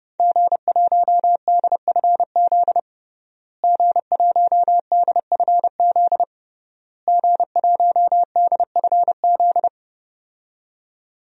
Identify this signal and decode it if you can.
Morse code, heard as a keyed tone, about 30 wpm, 700 Hz. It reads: G1DFZ G1DFZ G1DFZ
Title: DXCC-100 Call Signs - 5 Characters - Repeated 3x - 30wpm